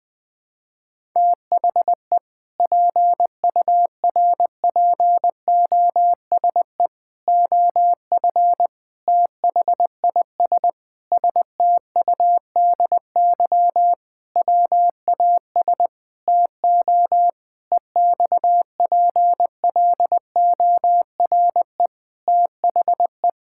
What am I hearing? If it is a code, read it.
Morse code, heard as a keyed tone, about 20 wpm, 700 Hz. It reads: THE PURPOSE OF THIS STUDY WAS TO EXPLORE THE
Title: Sentences with 5-Word Phrases - 20wpm